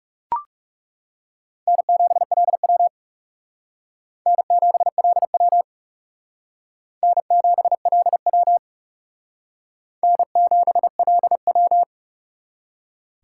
Morse code, heard as a keyed tone, about 39 wpm, 700 Hz. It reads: N7LW N7LW N7LW N7LW